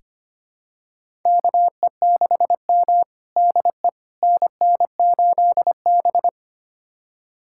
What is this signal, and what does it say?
Morse code, heard as a keyed tone, about 25 wpm, 700 Hz. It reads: KE6M DE NN8B
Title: USA Callsign DE Callsign - 4 Characters - 25wpm